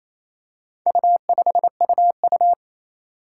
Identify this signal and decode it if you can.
Morse code, heard as a keyed tone, about 28 wpm, 700 Hz. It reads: U5UU
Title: Four Challenging Letter-Number Combinations 28wpm